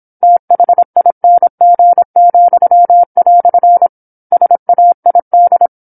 Morse code, heard as a keyed tone, about 27 wpm, 700 Hz. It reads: THING," SAID
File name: 08-Chapter-08 26 WPM